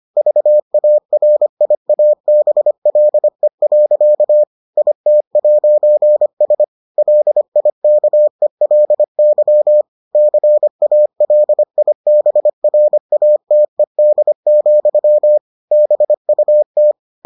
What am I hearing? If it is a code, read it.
Morse code, heard as a keyed tone, about 25 wpm, 600 Hz. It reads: VARIABLE. IT'S LIKELY CALIBRATED, BUT